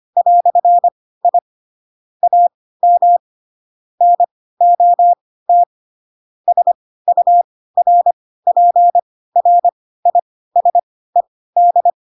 Morse code, heard as a keyed tone, about 25 wpm, spaced as for 16 wpm, 700 Hz. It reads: "I AM NOT SURPRISED
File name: Pinocchio / Pinocchio ch3 10 25 ik1hge